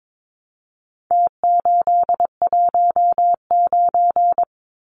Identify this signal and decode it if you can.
Morse code, heard as a keyed tone, about 22 wpm, 700 Hz. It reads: T819
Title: Numbers - Contest Style - 4-Digit 22wpm